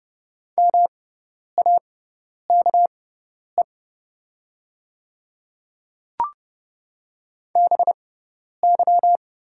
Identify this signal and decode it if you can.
Morse code, heard as a keyed tone, about 30 wpm, spaced as for 10 wpm, 700 Hz. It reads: MAKE BY
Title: 6x Character Spacing - Top 200 Words - 30wpm